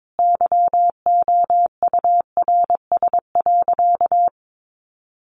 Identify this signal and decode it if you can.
Morse code, heard as a keyed tone, about 22 wpm, 700 Hz. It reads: YOURS.
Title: A Sound Of Thunder 22wpm